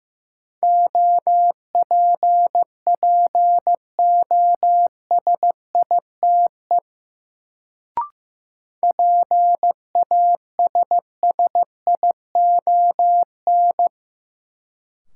Morse code, heard as a keyed tone, about 15 wpm, 700 Hz. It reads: OPPOSITE PASSION
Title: Sentences from Top 1300 Words - New Word Drill - 15wpm